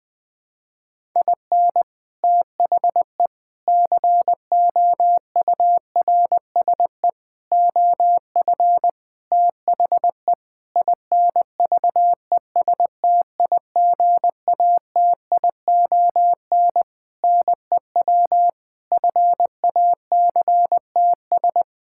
Morse code, heard as a keyed tone, about 20 wpm, 700 Hz. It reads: IN THE COURSE OF THE INVESTIGATION NEW FACTS
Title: Sentences with 5-Word Phrases - Spoken Beginning - 20wpm